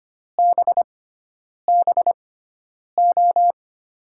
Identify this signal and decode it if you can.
Morse code, heard as a keyed tone, about 25 wpm, spaced as for 9 wpm, 700 Hz. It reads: BBO
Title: USA Call Signs - 5 Characters - 6x Character Spacing - 25wpm